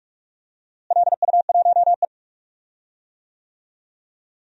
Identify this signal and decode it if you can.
Morse code, heard as a keyed tone, about 41 wpm, 700 Hz. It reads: LU1E
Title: DXCC-100 Call Signs - 4 Characters - Repeated 3x - 45wpm